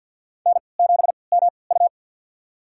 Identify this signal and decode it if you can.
Morse code, heard as a keyed tone, about 52 wpm, spaced as for 29 wpm, 700 Hz. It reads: N7MU
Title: USA Call Signs - 4 Characters - 3x Character Spacing - 50wpm